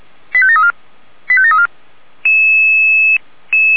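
Five-tone selective call (ZVEI), heard as quick runs of five tones, tones of about 70 ms. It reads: 85734 85734
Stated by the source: ZVEI five-tone selective call received by SDR, archived by Feuerwehr Orlamünde.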